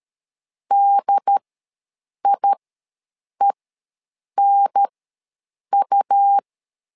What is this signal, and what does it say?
Morse code, heard as a keyed tone, about 13 wpm, spaced as for 7 wpm, 800 Hz. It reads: DIENU
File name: LEZIONI / 3 MWRO / 3 MW ws 12wpm 1816